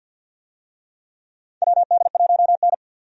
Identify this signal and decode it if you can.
Morse code, heard as a keyed tone, about 50 wpm, 700 Hz. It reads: WD1N
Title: USA Call Signs - 4 Characters - 50wpm